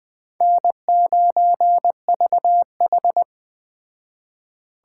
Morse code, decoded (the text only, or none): N9VH